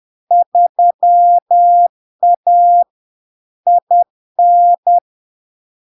Morse code, heard as a keyed tone, about 10 wpm, 700 Hz. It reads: 3A IN
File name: LARK-FD-2022-Log-in-CW-10wpm-700-Hz